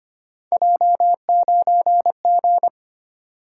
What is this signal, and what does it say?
Morse code, heard as a keyed tone, about 25 wpm, 700 Hz. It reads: J9G
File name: Three Letter-Number Combinations - 25wpm